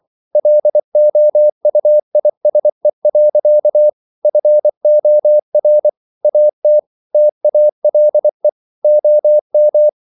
Morse code, heard as a keyed tone, about 24 wpm, 600 Hz. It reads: LOUISE. FOR AT TALE OM
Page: Ferie 24 wpm